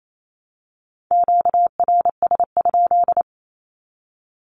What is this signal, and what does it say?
Morse code, heard as a keyed tone, about 28 wpm, 700 Hz. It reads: QRS?
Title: Q-Codes 28wpm